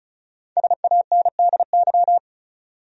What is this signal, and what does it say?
Morse code, heard as a keyed tone, about 34 wpm, 700 Hz. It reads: SANDY
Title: USA Common Names - 35wpm